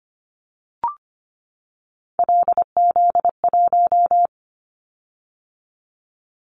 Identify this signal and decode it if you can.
Morse code, heard as a keyed tone, about 25 wpm, 700 Hz. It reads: LZ1
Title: Callsign Prefixes for Top 100 DXCC - 25wpm